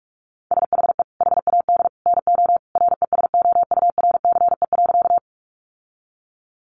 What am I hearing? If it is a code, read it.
Morse code, heard as a keyed tone, about 46 wpm, 700 Hz. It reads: SHE HAD NO RESOURCE.
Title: Sentences from Top 1300 Words 45wpm